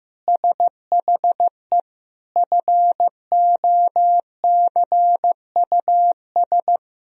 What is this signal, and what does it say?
Morse code, heard as a keyed tone, about 16 wpm, 700 Hz. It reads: SHE FOCUS
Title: Sentences from Top 1000 Words - Review Part 4 - 15wpm